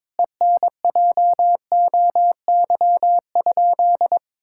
Morse code, heard as a keyed tone, about 22 wpm, 700 Hz. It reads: ENJOY?
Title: Sentences from Top 1000 Words - Review Part 5 - 22wpm